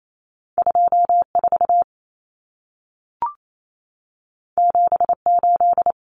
Morse code, heard as a keyed tone, about 28 wpm, 700 Hz. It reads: 24 78
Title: Numbers - 2-Digit 28wpm